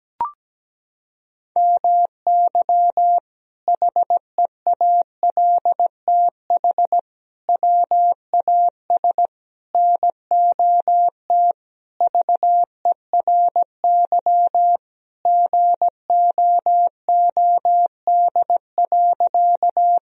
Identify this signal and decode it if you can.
Morse code, heard as a keyed tone, about 17 wpm, 700 Hz. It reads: MY HEALTH WAS NOT VERY GOOD.
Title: Sentences from Top 1000 Words - Review Part 3 - 17wpm